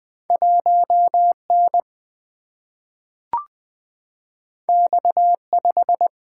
Morse code, heard as a keyed tone, about 20 wpm, 700 Hz. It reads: 1 N X 5
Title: Single-Letter Single-Number 20wpm